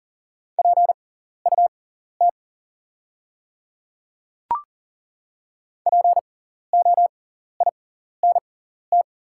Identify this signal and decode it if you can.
Morse code, heard as a keyed tone, about 41 wpm, spaced as for 14 wpm, 700 Hz. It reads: PUT POINT
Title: 6x Character Spacing - Top 500 Words - 40wpm